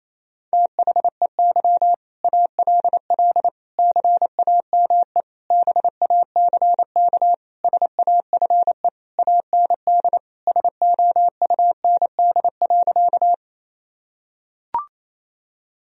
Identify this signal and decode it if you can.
Morse code, heard as a keyed tone, about 28 wpm, 700 Hz. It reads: THEY ALL CAME BACK SAFE AND SOUND.
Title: Sentences from Top 1000 Words - Review Part 1 - 28wpm